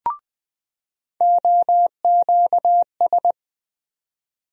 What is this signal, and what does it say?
Morse code, heard as a keyed tone, about 20 wpm, 700 Hz. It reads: OQS
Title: USA Call Sign Postfixes - 3 Letters 20wpm